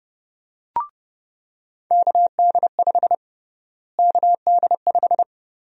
Morse code, heard as a keyed tone, about 32 wpm, 700 Hz. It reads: KD5 KD5
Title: USA Call Sign Prefixes - Repeated 2x - 30wpm